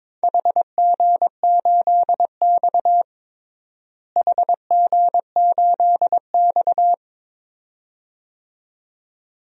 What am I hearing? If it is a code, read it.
Morse code, heard as a keyed tone, about 22 wpm, 700 Hz. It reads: HG8X HG8X
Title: DXCC-100 Call Signs - 4 Characters - Repeated 2x - 22wpm